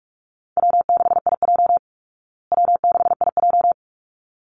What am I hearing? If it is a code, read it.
Morse code, heard as a keyed tone, about 44 wpm, 700 Hz. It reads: W6IJ W6IJ
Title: Easy Call Signs Repeated 2x - 45wpm